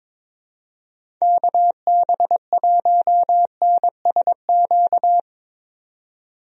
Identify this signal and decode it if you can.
Morse code, heard as a keyed tone, about 22 wpm, 700 Hz. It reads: KB1NSQ
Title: USA Call Signs - 6 Characters - 22wpm